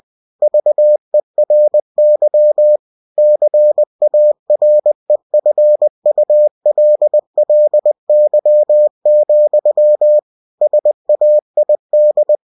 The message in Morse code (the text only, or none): VERY CAREFULLY, SAID